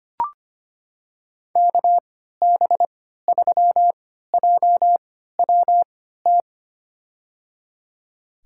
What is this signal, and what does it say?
Morse code, heard as a keyed tone, about 25 wpm, spaced as for 14 wpm, 700 Hz. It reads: KB3JWT
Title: USA Call Signs - 6 Characters - 3x Character Spacing - 25wpm